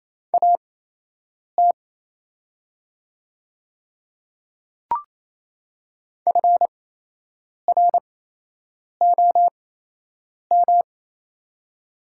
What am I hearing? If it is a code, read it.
Morse code, heard as a keyed tone, about 27 wpm, spaced as for 8 wpm, 700 Hz. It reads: AT FROM
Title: 8x Character Spacing - Top 200 Words - 28wpm